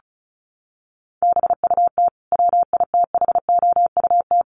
Morse code, heard as a keyed tone, about 34 wpm, 700 Hz. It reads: BUT WITHOUT
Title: Sentences from Top 1400 Words 35wpm